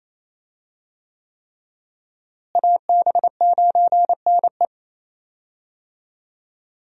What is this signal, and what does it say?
Morse code, heard as a keyed tone, about 28 wpm, 700 Hz. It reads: AB9NE
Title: USA Call Signs - 5 Characters - Repeated 3x - 28wpm